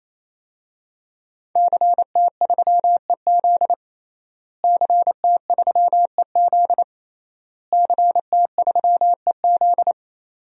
Morse code, heard as a keyed tone, about 27 wpm, 700 Hz. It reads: CT3EZ CT3EZ CT3EZ
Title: DXCC-100 Call Signs - 5 Characters - Repeated 3x - 28wpm